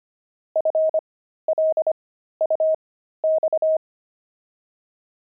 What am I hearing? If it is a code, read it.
Morse code, heard as a keyed tone, about 25 wpm, spaced as for 13 wpm, 630 Hz. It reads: FLUX
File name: pangram-fs-13-cs-25-jelly